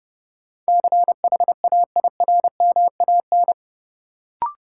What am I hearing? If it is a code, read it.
Morse code, heard as a keyed tone, about 32 wpm, 700 Hz. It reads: CHAIRMAN
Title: Sets of 3 Words - New Word Drill - 30wpm